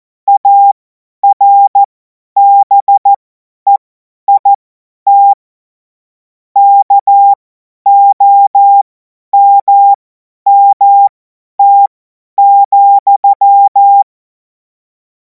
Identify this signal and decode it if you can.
Morse code, heard as a keyed tone, about 14 wpm, spaced as for 10 wpm, 800 Hz. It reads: ARBEIT KOMMT,
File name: 50 F70